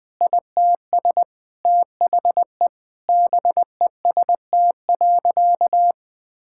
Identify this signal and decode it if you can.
Morse code, heard as a keyed tone, about 20 wpm, 700 Hz. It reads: ITS THE BEST.
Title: Sentences with 3-Word Phrases - 20wpm